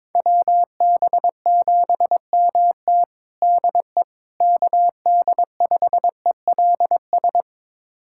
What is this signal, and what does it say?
Morse code, heard as a keyed tone, about 22 wpm, 700 Hz. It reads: WB7MT DE KD5ELS
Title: USA Callsign DE Callsign 22wpm